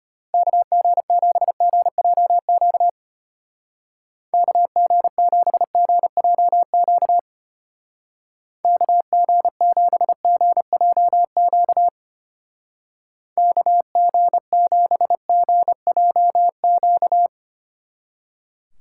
Morse code, heard as a keyed tone, about 32 wpm, 700 Hz. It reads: KG7GJQ KG7GJQ KG7GJQ KG7GJQ